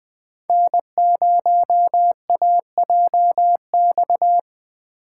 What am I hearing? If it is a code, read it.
Morse code, heard as a keyed tone, about 20 wpm, 700 Hz. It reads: N0AJX
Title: Speed Racing - USA 1x3 Call Signs 17wpm